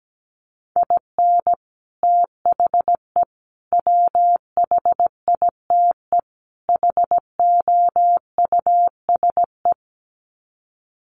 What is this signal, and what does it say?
Morse code, heard as a keyed tone, about 17 wpm, 700 Hz. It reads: IN THE WHITE HOUSE
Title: Sets of 4 Words 17wpm